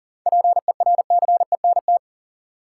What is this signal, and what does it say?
Morse code, heard as a keyed tone, about 40 wpm, 700 Hz. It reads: PERCENT
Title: Sets of 2 Words - New Word Drill - 40wpm